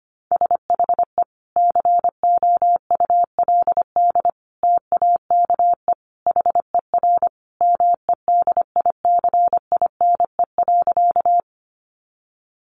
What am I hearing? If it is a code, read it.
Morse code, heard as a keyed tone, about 25 wpm, 700 Hz. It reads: SHE COULD TAKE HER MEDICINE.
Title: Sentences from Top 900 Words 25wpm